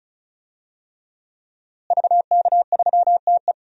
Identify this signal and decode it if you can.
Morse code, heard as a keyed tone, about 34 wpm, 700 Hz. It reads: VK3TE
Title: DXCC-100 Call Signs - 5 Characters - Repeated 3x - 35wpm